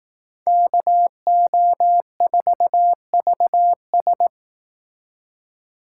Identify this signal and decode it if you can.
Morse code, heard as a keyed tone, about 18 wpm, 700 Hz. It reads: KO4VS
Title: Speed Racing - USA Call Signs 15wpm